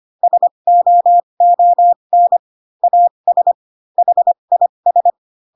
Morse code, heard as a keyed tone, about 25 wpm, spaced as for 22 wpm, 700 Hz. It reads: SOON AS HIS
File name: Pinocchio / Pinocchio ch8 18 25 ik1hge